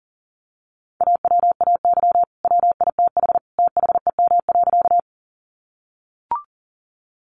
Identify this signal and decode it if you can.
Morse code, heard as a keyed tone, about 40 wpm, 700 Hz. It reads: AWAY WITH THEM.
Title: Sentences from Top 600 Words 40wpm